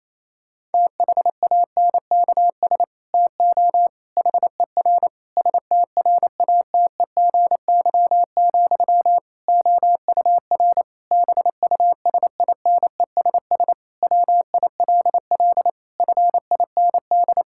Morse code, heard as a keyed tone, about 28 wpm, 700 Hz. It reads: THANKS TO HER STRATEGY, OUR BUSINESS WILL FIND
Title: Sentences from Top 1000 Words - Review Part 5 - 28wpm